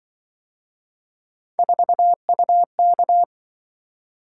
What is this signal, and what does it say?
Morse code, heard as a keyed tone, about 25 wpm, 700 Hz. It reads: 4UK
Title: Speed-Racing - Three Letter-Number Combinations - 20wpm